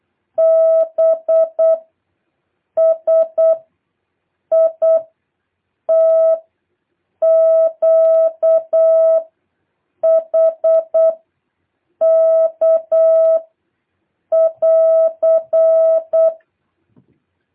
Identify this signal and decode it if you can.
Morse code, heard as a keyed tone, about 8 wpm, 650 Hz. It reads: B S I T Q H K <AR>
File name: jh8cbh / 2023cw